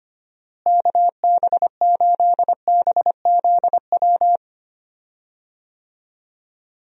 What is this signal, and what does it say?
Morse code, heard as a keyed tone, about 25 wpm, 700 Hz. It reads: KB8BZW
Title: USA Call Signs - 6 Characters - Repeated 3x - 25wpm